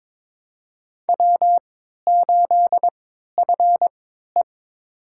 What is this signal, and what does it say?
Morse code, heard as a keyed tone, about 22 wpm, spaced as for 12 wpm, 700 Hz. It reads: W8FE
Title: USA Call Signs - 4 Characters - 3x Character Spacing - 22wpm